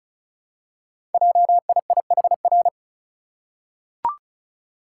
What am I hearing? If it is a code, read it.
Morse code, heard as a keyed tone, about 36 wpm, 700 Hz. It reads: JIIHR
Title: Code Groups - 5 Letters - 35wpm